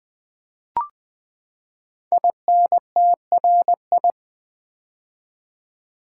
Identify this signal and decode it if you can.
Morse code, heard as a keyed tone, about 20 wpm, 700 Hz. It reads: INTRI
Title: Build A Word - 5 Letters At A Time - 10-Letter Words - 20wpm